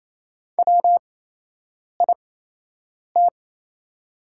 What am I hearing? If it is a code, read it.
Morse code, heard as a keyed tone, about 28 wpm, spaced as for 8 wpm, 700 Hz. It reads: WIT
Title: 8x Character Spacing - Top 100 Words - 28wpm